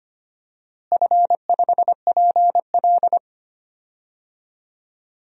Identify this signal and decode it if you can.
Morse code, heard as a keyed tone, about 25 wpm, 700 Hz. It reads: F5PL
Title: DXCC-100 Call Signs - 4 Characters - Repeated 2x - 25wpm